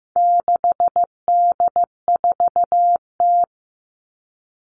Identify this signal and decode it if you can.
Morse code, heard as a keyed tone, about 15 wpm, 700 Hz. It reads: 6D4T
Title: All Cut Numbers - 4-Digit 15wpm